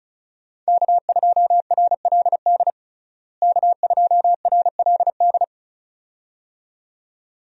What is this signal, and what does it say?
Morse code, heard as a keyed tone, about 34 wpm, 700 Hz. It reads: K2RLD K2RLD
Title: USA Call Signs - 5 Characters - Repeated 2x - 35wpm